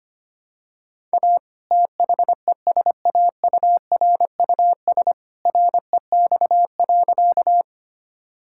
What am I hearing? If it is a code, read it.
Morse code, heard as a keyed tone, about 25 wpm, 700 Hz. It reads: A THESAURUS REX.
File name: practice / 25wpm / All-Age Appropriate Jokes - 25wpm